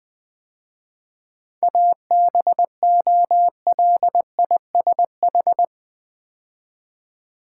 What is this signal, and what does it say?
Morse code, heard as a keyed tone, about 20 wpm, 690 Hz. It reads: ABOLISH
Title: 7 Letter Words - First 3 Letters Given - 20wpm